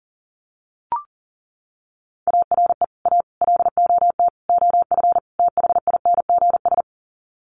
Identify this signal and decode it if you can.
Morse code, heard as a keyed tone, about 40 wpm, 700 Hz. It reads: ARE A LOT OF THINGS
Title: Sets of 5 Words Encore2 40wpm